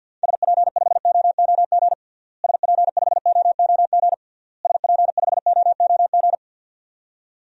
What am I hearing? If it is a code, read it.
Morse code, heard as a keyed tone, about 52 wpm, 700 Hz. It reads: SP5OOG SP5OOG SP5OOG
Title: DXCC-100 Call Signs - 6 Characters - Repeated 3x - 50wpm